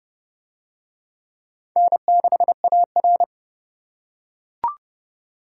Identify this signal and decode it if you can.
Morse code, heard as a keyed tone, about 27 wpm, 700 Hz. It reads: N6AR